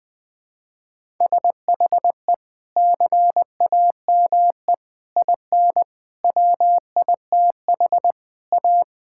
Morse code, heard as a keyed tone, about 20 wpm, 700 Hz. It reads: SHE CAME IN WITH A
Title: Sentences with 2-Word Phrases - 20wpm